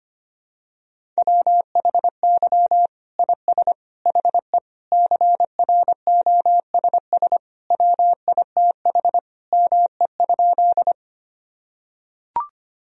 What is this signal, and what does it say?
Morse code, heard as a keyed tone, about 25 wpm, 700 Hz. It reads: WHY IS HE CROSS WITH ME?
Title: Sentences from Top 1000 Words - Review Part 1 - 25wpm